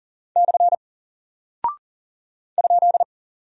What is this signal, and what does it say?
Morse code, heard as a keyed tone, about 40 wpm, 700 Hz. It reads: / ?
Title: Single Character (Letter:Number:Punctuation:Prosign) Alt 40wpm